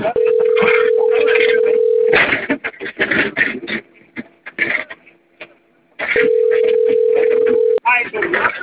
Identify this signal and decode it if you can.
Ringback tone (North America), 440 + 480 Hz, with speech.
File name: out- - - - - .39265